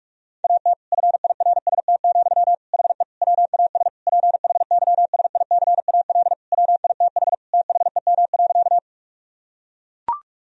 Morse code, heard as a keyed tone, about 42 wpm, 700 Hz. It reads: AT FIRST, HE WAS PHYSICAL WITH THEM.
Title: Sentences from Top 1000 Words - Review Part 4 - 45wpm